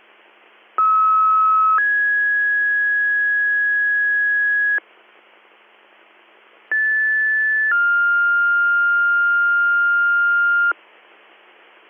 Two-tone page, heard as a one-second tone, then a three-second tone, about 1290 then 1650 Hz, about 1750 then 1400 Hz. It is heard from 0.8 to 10.7 s.